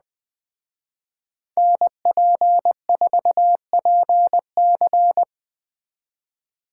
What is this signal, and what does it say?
Morse code, heard as a keyed tone, about 20 wpm, 700 Hz. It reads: NP4PC
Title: Speed Racing - USA 2x2 Call Signs 17wpm